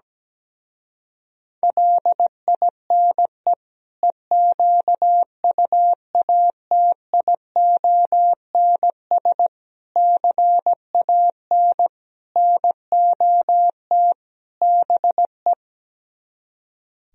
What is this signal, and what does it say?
Morse code, heard as a keyed tone, about 17 wpm, 700 Hz. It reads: LINE EQUATIONS CAN NOT BE
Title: Speed-Racing - Sets of 5 Words 17wpm